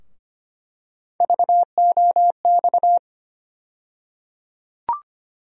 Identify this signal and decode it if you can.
Morse code, heard as a keyed tone, about 25 wpm, 700 Hz. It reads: VOX
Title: USA Call Sign Postfixes - 3 Letters 25wpm